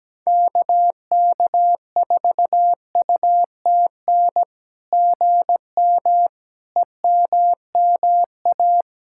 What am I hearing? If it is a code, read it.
Morse code, heard as a keyed tone, about 17 wpm, 660 Hz. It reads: KK4UTN GM EMMA
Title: SOTA QSO Practice 17wpm